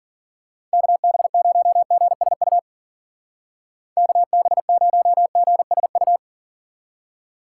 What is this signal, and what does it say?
Morse code, heard as a keyed tone, about 46 wpm, 700 Hz. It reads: KB0GSU KB0GSU
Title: Speed Racing - USA 2x3 Call Signs 40wpm